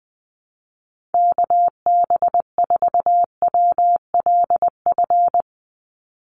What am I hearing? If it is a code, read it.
Morse code, heard as a keyed tone, about 19 wpm, 700 Hz. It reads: KB4WLF
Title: Call Signs - 20wpm